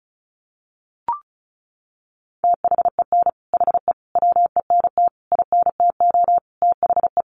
Morse code, heard as a keyed tone, about 34 wpm, 700 Hz. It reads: THEN HE WENT INTO THE
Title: Sentences from Top 1500 Words 35wpm